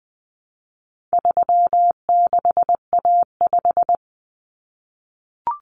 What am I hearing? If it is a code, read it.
Morse code, heard as a keyed tone, about 20 wpm, 700 Hz. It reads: 36A5
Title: Common Cut Numbers - 4-Digit 20wpm